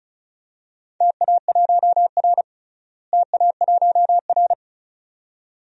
Morse code, heard as a keyed tone, about 36 wpm, 700 Hz. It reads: TA1R TA1R
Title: DXCC-100 Call Signs - 4 Characters - Repeated 2x - 35wpm